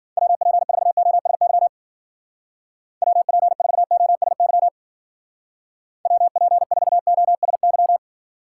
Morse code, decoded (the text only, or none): WP4OSY WP4OSY WP4OSY